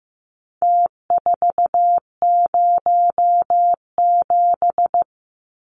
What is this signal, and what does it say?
Morse code, heard as a keyed tone, about 15 wpm, 700 Hz. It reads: T407
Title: Numbers - Contest Style - 4-Digit 15wpm